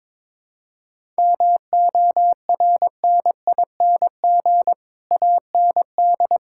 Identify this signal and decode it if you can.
Morse code, heard as a keyed tone, about 21 wpm, 700 Hz. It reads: MORNING AND
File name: Sets of 9 Words - 22wpm